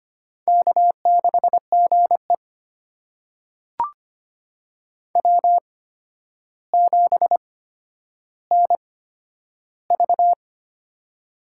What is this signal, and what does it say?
Morse code, heard as a keyed tone, about 25 wpm, spaced as for 11 wpm, 700 Hz. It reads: K6GE W7NV
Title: USA Call Signs - 4 Characters - 8x Character Spacing - 25wpm